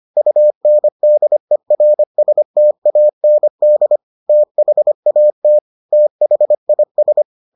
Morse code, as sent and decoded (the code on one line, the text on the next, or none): ..- -. -.. .-. ... - .- -. -.. / - .... .- - / - .... .. ...
UNDERSTAND THAT THIS